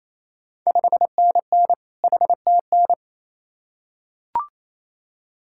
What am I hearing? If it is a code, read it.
Morse code, heard as a keyed tone, about 28 wpm, 700 Hz. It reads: 5NN HTN
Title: Ontario QSO Party Exchange 28wpm